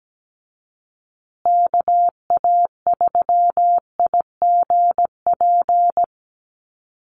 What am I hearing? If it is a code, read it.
Morse code, heard as a keyed tone, about 17 wpm, 700 Hz. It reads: KA3IGP
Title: USA Call Signs Repeated 2x 17wpm